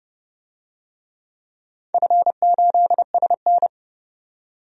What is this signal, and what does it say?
Morse code, heard as a keyed tone, about 30 wpm, 700 Hz. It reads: F8SN
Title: DXCC-100 Call Signs - 4 Characters - 30wpm